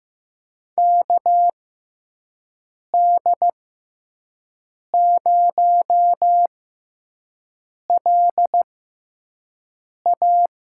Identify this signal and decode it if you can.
Morse code, heard as a keyed tone, about 15 wpm, spaced as for 5 wpm, 700 Hz. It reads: KD0LA
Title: USA Call Signs - 6 Characters - 6x Character Spacing - 15wpm